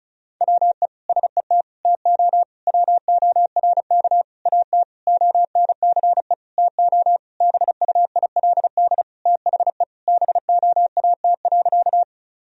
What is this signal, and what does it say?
Morse code, heard as a keyed tone, about 36 wpm, 700 Hz. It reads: WE SET TO WORK AT ONCE TO BUILD THE BOAT.
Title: Sentences from Top 1000 Words - Review Part 1 - 35wpm